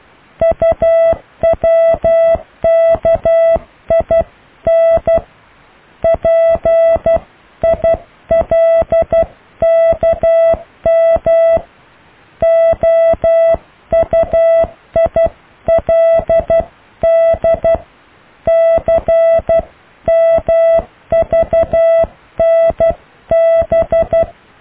Morse code, heard as a keyed tone, about 12 wpm, 660 Hz. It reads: UWKIN PILKM OUILD CMVNB
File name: cw / IJLUM+CDRO+EKTPV+BFHNW